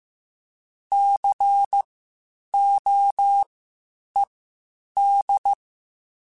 Morse code, heard as a keyed tone, about 15 wpm, spaced as for 8 wpm, 800 Hz. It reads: COED
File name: Spaced Alphas / Spaced Alphas mp3s / Cs sp @13